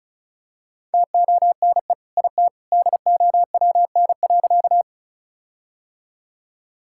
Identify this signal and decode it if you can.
Morse code, heard as a keyed tone, about 34 wpm, 700 Hz. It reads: TONE IT DOWN.
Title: Sentences from Top 1600 Words 35wpm